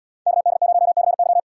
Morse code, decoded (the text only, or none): KD0CY